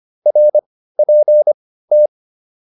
Morse code, heard as a keyed tone, about 25 wpm, spaced as for 15 wpm, 600 Hz. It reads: RPT